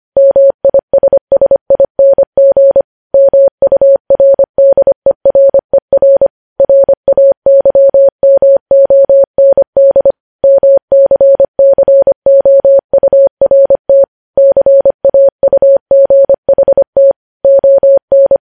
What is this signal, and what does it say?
Morse code, heard as a keyed tone, about 25 wpm, 550 Hz. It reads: MISSING MURDERER RAYMOND MCCOURT CAUGHT ON
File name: podcast / News headlines-1x-25x25